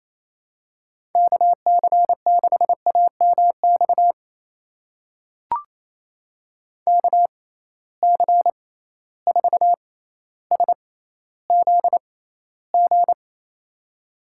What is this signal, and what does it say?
Morse code, heard as a keyed tone, about 28 wpm, spaced as for 14 wpm, 700 Hz. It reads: KC6AMX KC4SZG
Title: USA Call Signs - 6 Characters - 6x Character Spacing - 28wpm